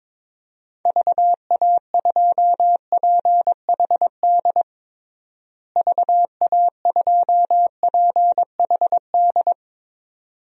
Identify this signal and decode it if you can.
Morse code, heard as a keyed tone, about 22 wpm, 700 Hz. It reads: VA2PHD VA2PHD
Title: DXCC-100 Call Signs - 6 Characters - Repeated 2x - 22wpm